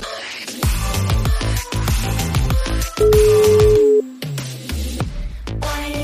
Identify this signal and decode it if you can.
Ringback tone (Europe, 425 Hz), with noise.